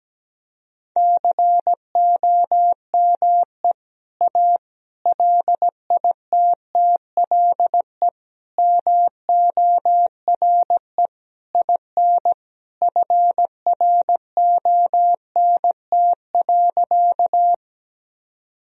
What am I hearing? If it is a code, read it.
Morse code, heard as a keyed tone, about 17 wpm, 700 Hz. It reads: COME A LITTLE MORE IN FRONT.
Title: Sentences from Top 1000 Words - Review Part 2 - 17wpm